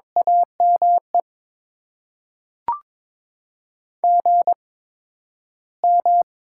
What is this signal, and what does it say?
Morse code, heard as a keyed tone, about 22 wpm, spaced as for 12 wpm, 700 Hz. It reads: AME GM
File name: Top 20 QSO Elements - 8x Character Spacing - 22wpm